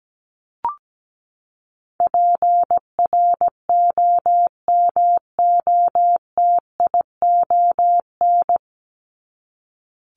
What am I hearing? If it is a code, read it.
Morse code, heard as a keyed tone, about 17 wpm, 700 Hz. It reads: PROMOTION
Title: Top 1501-1750 Words 17wpm